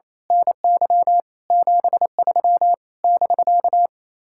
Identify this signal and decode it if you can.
Morse code, heard as a keyed tone, about 27 wpm, 700 Hz. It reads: NY 73 <BK>
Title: Speed-Racing Common QSO Phrases - 28wpm